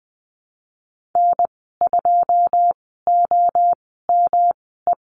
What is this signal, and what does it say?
Morse code, heard as a keyed tone, about 20 wpm, spaced as for 15 wpm, 700 Hz. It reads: N2OME
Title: 154 Lesson 25g - Letter p - Call Signs 20wpm